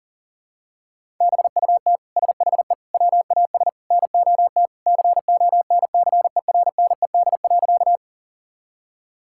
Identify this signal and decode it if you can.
Morse code, heard as a keyed tone, about 40 wpm, 700 Hz. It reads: BUT SHE WAS NOT CONCERNED.
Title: Sentences from Top 1000 Words - Review Part 6 - 40wpm